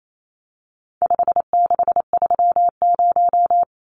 Morse code, heard as a keyed tone, about 28 wpm, 700 Hz. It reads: 5630
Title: Numbers - 4-Digit 28wpm